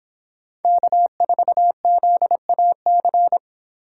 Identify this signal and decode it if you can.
Morse code, heard as a keyed tone, about 30 wpm, 700 Hz. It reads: K4ZAC